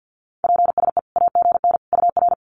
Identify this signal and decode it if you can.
Morse code, heard as a keyed tone, about 50 wpm, 700 Hz. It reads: PSE AGN UR
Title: Common QSO Phrases 50wpm